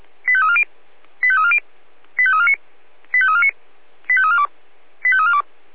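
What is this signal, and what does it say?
Five-tone selective call (ZVEI), heard as quick runs of five tones, tones of about 70 ms. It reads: 85340 85340 85349 85349 85342 85342
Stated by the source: SDR-received ZVEI five-tone selective call.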